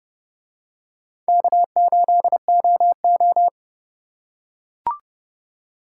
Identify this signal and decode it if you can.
Morse code, heard as a keyed tone, about 29 wpm, 700 Hz. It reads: K8OO